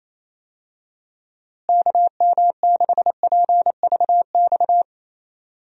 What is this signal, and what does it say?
Morse code, heard as a keyed tone, about 28 wpm, 700 Hz. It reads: KM6PVX
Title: USA Call Signs - 6 Characters - Repeated 3x - 28wpm